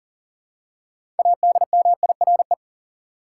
Morse code, heard as a keyed tone, about 40 wpm, 700 Hz. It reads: ADMIRE
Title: Six Letter Words - 40wpm